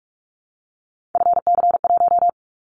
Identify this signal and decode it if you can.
Morse code, heard as a keyed tone, about 46 wpm, 700 Hz. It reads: FC1
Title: Callsign Prefixes for Top 25 DXCC - 45wpm